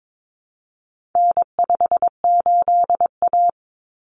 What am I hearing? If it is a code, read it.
Morse code, heard as a keyed tone, about 22 wpm, 700 Hz. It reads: N58A
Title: Common Cut Numbers - 4-Digit 22wpm